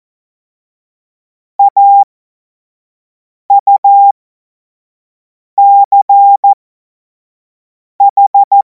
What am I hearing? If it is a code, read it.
Morse code, heard as a keyed tone, about 14 wpm, spaced as for 5 wpm, 800 Hz. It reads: AUCH
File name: Morsekurs / Witze / Witz 10. Februar 2026 25 F70